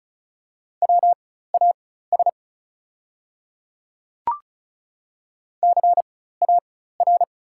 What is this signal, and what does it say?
Morse code, heard as a keyed tone, about 36 wpm, spaced as for 16 wpm, 690 Hz. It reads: WAS CAR
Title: 4x Character Spacing - Top 300 Words - 35wpm